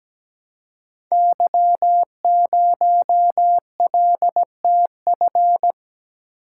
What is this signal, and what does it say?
Morse code, heard as a keyed tone, about 17 wpm, 700 Hz. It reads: Y0LTF
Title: Code Groups - 5 Letter:Numbers - 17wpm